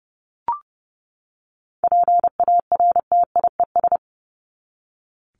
Morse code, heard as a keyed tone, about 30 wpm, 700 Hz. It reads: PARTIES
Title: ES:S Suffix - Top 100 Words with Base Words 30wpm